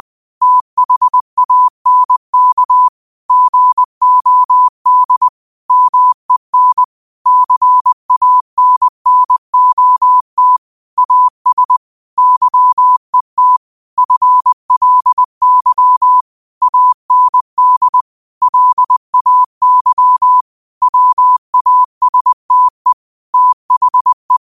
Morse code, heard as a keyed tone, about 20 wpm, 1000 Hz. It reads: THANK GOD MEN CANNOT AS YET FLY AND LAY WASTE THE
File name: qotdfiles / 20wpm